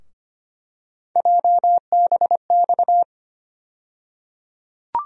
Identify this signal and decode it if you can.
Morse code, heard as a keyed tone, about 25 wpm, 700 Hz. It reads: JBX
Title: USA Call Sign Postfixes - 3 Letters 25wpm